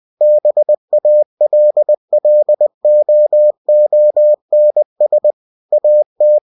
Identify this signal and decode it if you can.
Morse code, heard as a keyed tone, about 20 wpm, 600 Hz. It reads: BALLOONS AT
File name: cw-books / winnie-the-pooh / winnie-20wpm-page-0005